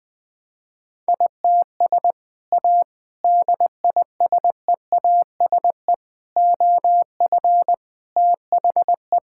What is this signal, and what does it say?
Morse code, heard as a keyed tone, about 20 wpm, 700 Hz. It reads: ITS A DISEASE OF THE
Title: Sentences from Top 1000 Words - Review Part 6 - 20wpm